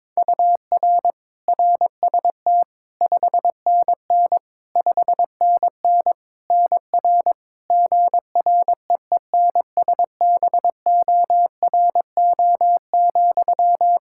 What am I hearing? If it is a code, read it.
Morse code, heard as a keyed tone, about 21 wpm, 700 Hz. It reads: UR RST 5NN 5NN NR GREENSBORO,
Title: Practice QSO - Exchange 1 - Beginning - 22wpm